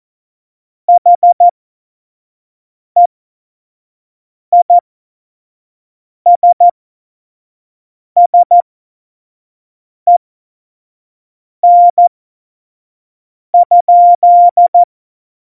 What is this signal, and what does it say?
Morse code, heard as a keyed tone, about 14 wpm, spaced as for 5 wpm, 700 Hz. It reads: HEISSEN?
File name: Morsekurs / Witze / Witz 01. Dezember, 2020 25 F70